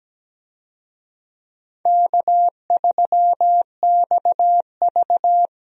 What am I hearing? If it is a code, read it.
Morse code, heard as a keyed tone, about 17 wpm, 700 Hz. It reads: K3XV
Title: USA Call Signs - 4 Characters - Repeated 2x - 17wpm